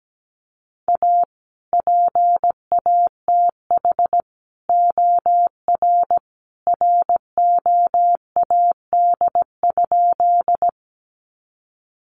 Morse code, heard as a keyed tone, about 17 wpm, 700 Hz. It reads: A PATH OR ROAD?
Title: Sentences from Top 1400 Words 17wpm